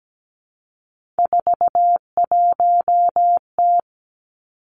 Morse code, heard as a keyed tone, about 17 wpm, 700 Hz. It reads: 41T
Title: Three Challenging Letter-Number Combinations 17wpm